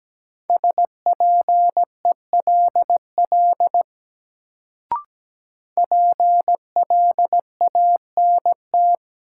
Morse code, heard as a keyed tone, about 17 wpm, 700 Hz. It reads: SPELL PLANT
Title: Sentences from Top 1000 Words - Review Part 1 - New Word Drill - 17wpm